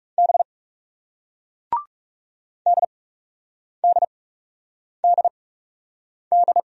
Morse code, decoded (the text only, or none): B DDDD